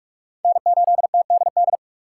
Morse code, heard as a keyed tone, about 55 wpm, 700 Hz. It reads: N8TDD
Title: Speed Racing - USA 1x3 Call Signs 45wpm